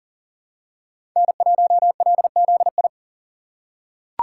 Morse code, heard as a keyed tone, about 40 wpm, 700 Hz. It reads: N1LZI
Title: USA Call Signs - 3x Character Spacing - 40wpm